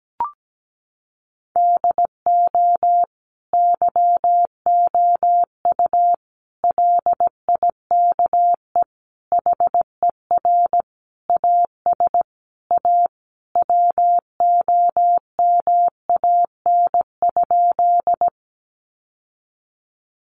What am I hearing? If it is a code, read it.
Morse code, heard as a keyed tone, about 17 wpm, 700 Hz. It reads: DO YOU LIKE HER AS A WOMAN?
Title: Sentences from Top 1100 Words 17wpm